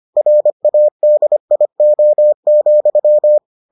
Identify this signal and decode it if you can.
Morse code, heard as a keyed tone, about 25 wpm, 600 Hz. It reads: RADIO,